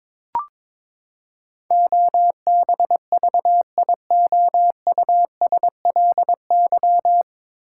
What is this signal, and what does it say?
Morse code, heard as a keyed tone, about 22 wpm, 700 Hz. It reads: OBVIOUSLY
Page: Morse Code Ninja practice 22wpm